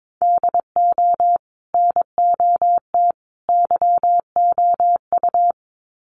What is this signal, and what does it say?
Morse code, heard as a keyed tone, about 22 wpm, 700 Hz. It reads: DO NOT YOU
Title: Sets of 3 Words 22wpm